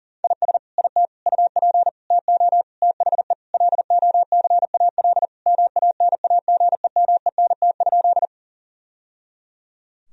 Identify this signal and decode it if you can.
Morse code, heard as a keyed tone, about 40 wpm, 700 Hz. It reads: IS IT UP TO THE LOCAL MANAGEMENT?
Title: Sentences from Top 1000 Words - Review Part 4 - 40wpm